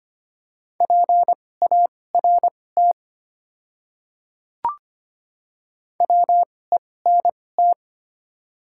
Morse code, heard as a keyed tone, about 25 wpm, spaced as for 18 wpm, 700 Hz. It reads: PART WENT
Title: 2x Character Spacing - Top 200 Words - 25wpm